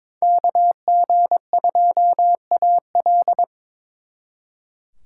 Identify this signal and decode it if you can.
Morse code, heard as a keyed tone, about 26 wpm, 700 Hz. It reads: KG2AL
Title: Speed Racing - USA 2x2 Call Signs 22wpm